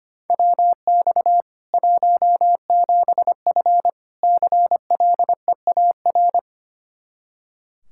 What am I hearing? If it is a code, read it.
Morse code, heard as a keyed tone, about 25 wpm, 700 Hz. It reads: WX 17F CLEAR
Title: Practice QSO - Exchange 2 - Info - 25wpm